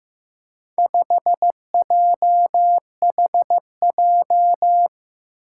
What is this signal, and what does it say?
Morse code, heard as a keyed tone, about 15 wpm, 700 Hz. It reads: 5JHJ
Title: Four Challenging Letter-Number Combinations 15wpm